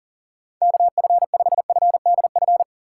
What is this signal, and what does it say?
Morse code, heard as a keyed tone, about 40 wpm, 700 Hz. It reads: KF5FDF
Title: USA Call Signs - 6 Characters - 2x Character Spacing - 40wpm